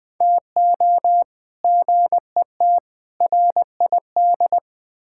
Morse code, heard as a keyed tone, about 20 wpm, 700 Hz. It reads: TO GET RID
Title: Build A Phrase - 4-Word Phrases - 20wpm